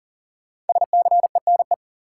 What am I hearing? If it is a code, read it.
Morse code, heard as a keyed tone, about 40 wpm, 700 Hz. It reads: SCENE
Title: Sentences from Top 1100 Words - New Word Drill - 40wpm